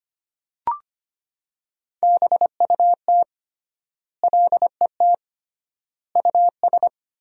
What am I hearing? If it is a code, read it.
Morse code, heard as a keyed tone, about 25 wpm, 700 Hz. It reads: BUT LET US
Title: 3x Word-Spacing - Sentences from Top 300 Words - 25wpm